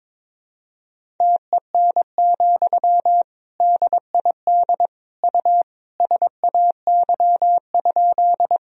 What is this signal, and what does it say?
Morse code, heard as a keyed tone, about 22 wpm, 700 Hz. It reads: TEN, DID U SAY?
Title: Sentences from Top 400 Words and Abbreviations 22wpm